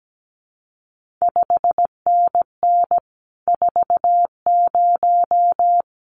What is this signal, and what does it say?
Morse code, heard as a keyed tone, about 17 wpm, 700 Hz. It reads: 5NN 40
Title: CQ WW DX Contest Exchange Encore - 17wpm